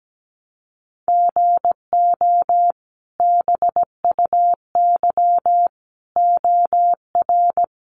Morse code, heard as a keyed tone, about 17 wpm, 700 Hz. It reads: GO BUY OR
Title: Sentences from Top 1500 Words 17wpm